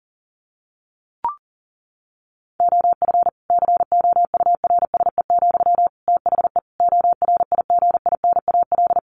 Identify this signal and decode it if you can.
Morse code, heard as a keyed tone, about 37 wpm, 700 Hz. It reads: OF COURSE, THE ORIGINAL
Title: Sentences from Top 1500 Words 40wpm